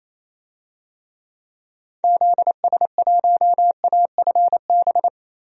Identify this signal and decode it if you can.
Morse code, heard as a keyed tone, about 28 wpm, 700 Hz. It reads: ZS1AFB
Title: DXCC-100 Call Signs - 6 Characters - Repeated 3x - 28wpm